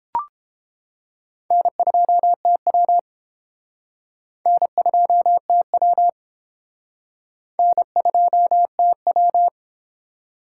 Morse code, heard as a keyed tone, about 29 wpm, 700 Hz. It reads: N2TW N2TW N2TW